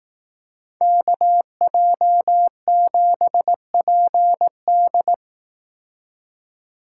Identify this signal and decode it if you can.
Morse code, heard as a keyed tone, about 18 wpm, 700 Hz. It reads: KJ7PD
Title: Speed Racing - USA 2x2 Call Signs 15wpm